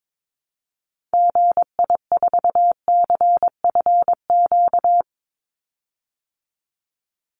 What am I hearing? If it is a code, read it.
Morse code, heard as a keyed tone, about 22 wpm, 700 Hz. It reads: GI4CFQ
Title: Call Signs Repeated 3x - 22wpm